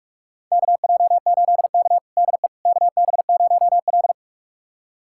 Morse code, heard as a keyed tone, about 45 wpm, 700 Hz. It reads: KJ8K DE KB0L
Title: USA Callsign DE Callsign - 4 Characters - 45wpm